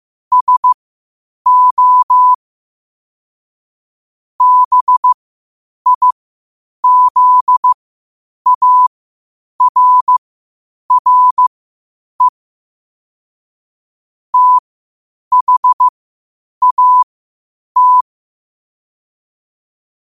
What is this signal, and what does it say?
Morse code, heard as a keyed tone, about 15 wpm, spaced as for 8 wpm, 1000 Hz. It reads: SO BIZARRE THAT
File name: qotdfiles / 8wpm